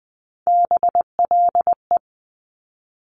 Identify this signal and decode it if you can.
Morse code, heard as a keyed tone, about 19 wpm, 700 Hz. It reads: BLE
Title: Top Three Letter Combinations in Words 20wpm